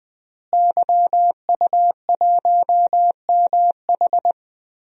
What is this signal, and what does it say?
Morse code, heard as a keyed tone, about 20 wpm, 700 Hz. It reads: YU1MH
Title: DXCC-100 Call Signs - 5 Characters - Repeated 3x - 20wpm